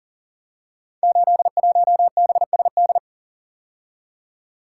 Morse code, heard as a keyed tone, about 39 wpm, 700 Hz. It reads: 81BSD
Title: Code Groups - 5 Letter:Numbers - 40wpm